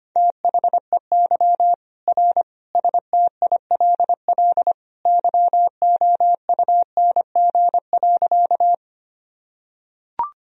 Morse code, heard as a keyed tone, about 25 wpm, 700 Hz. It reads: THEY R STILL YOUNG.
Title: Sentences from Top 400 Words and Abbreviations 25wpm